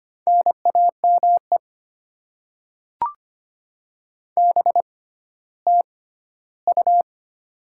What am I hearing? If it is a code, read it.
Morse code, heard as a keyed tone, about 25 wpm, spaced as for 14 wpm, 700 Hz. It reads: NAME BTU
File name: Top 20 QSO Elements - 6x Character Spacing - 25wpm